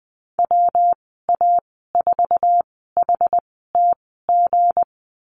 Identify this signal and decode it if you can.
Morse code, heard as a keyed tone, about 20 wpm, spaced as for 14 wpm, 700 Hz. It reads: WA4HTG
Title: 161 Lesson 26g - Letter g - Call Signs 20wpm